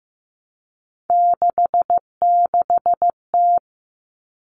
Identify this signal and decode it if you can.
Morse code, heard as a keyed tone, about 15 wpm, 700 Hz. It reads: 66T
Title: Common Cut Numbers - 3-Digit 15wpm